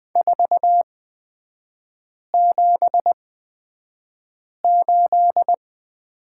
Morse code, heard as a keyed tone, about 20 wpm, spaced as for 5 wpm, 700 Hz. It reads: 478
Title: 243 Lesson 39e - Letter z - Three Characters 20wpm